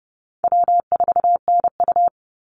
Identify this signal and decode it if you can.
Morse code, heard as a keyed tone, about 30 wpm, 700 Hz. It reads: W4NU
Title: Easy Call Signs - 30wpm